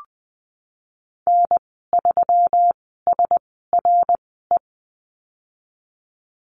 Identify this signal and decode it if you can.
Morse code, heard as a keyed tone, about 20 wpm, spaced as for 15 wpm, 700 Hz. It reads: N3SRE
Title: 122 Lesson 20g - Number 3 - Call Signs 20wpm